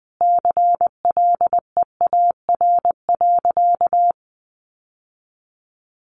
Morse code, heard as a keyed tone, about 20 wpm, 700 Hz. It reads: CLEAR.
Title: Sentences from Top 1300 Words 20wpm